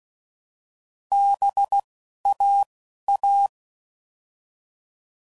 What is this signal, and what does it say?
Morse code, heard as a keyed tone, about 16 wpm, 800 Hz. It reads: B A A